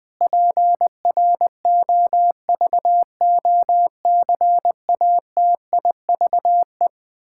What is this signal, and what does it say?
Morse code, heard as a keyed tone, about 20 wpm, 700 Hz. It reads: PROVOCATIVE
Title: Pangram Challenge v2 - 20wpm